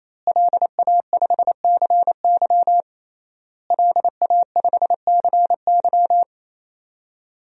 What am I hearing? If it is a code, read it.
Morse code, heard as a keyed tone, about 28 wpm, 700 Hz. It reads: LA5CY LA5CY
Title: DXCC-100 Call Signs - 5 Characters - Repeated 3x - 28wpm